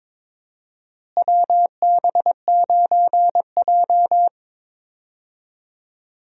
Morse code, heard as a keyed tone, about 22 wpm, 700 Hz. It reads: WB9J
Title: USA Call Signs - 4 Characters - Repeated 3x - 22wpm